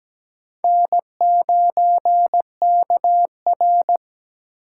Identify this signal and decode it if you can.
Morse code, heard as a keyed tone, about 17 wpm, 700 Hz. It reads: N9KR
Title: K1USN SST QSO Practice - 17wpm